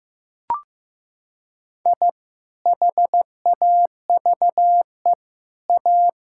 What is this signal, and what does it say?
Morse code, heard as a keyed tone, about 15 wpm, 700 Hz. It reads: I HAVE A
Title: Sentences from Top 1000 Words - Review Part 6 - 15wpm